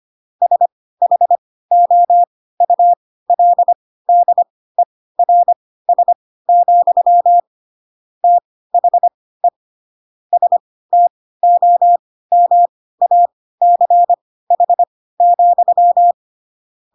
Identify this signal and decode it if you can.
Morse code, heard as a keyed tone, about 25 wpm, spaced as for 16 wpm, 700 Hz. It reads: SHOULDERS, THE STOMACH,
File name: Pinocchio / Pinocchio ch3 10 25 ik1hge